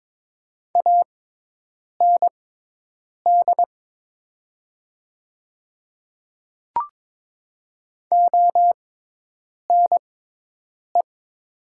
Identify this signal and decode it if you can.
Morse code, heard as a keyed tone, about 21 wpm, spaced as for 8 wpm, 700 Hz. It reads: AND ONE
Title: 6x Character Spacing - Top 300 Words - 22wpm